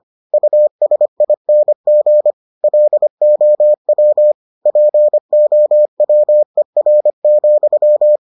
Morse code, heard as a keyed tone, about 25 wpm, 600 Hz. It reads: USING LOW POWER,